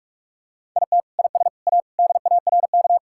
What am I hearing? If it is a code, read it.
Morse code, heard as a keyed tone, about 46 wpm, 700 Hz. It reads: IT IS A DARK
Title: Sentences from Top 1000 Words - Review Part 5 - 45wpm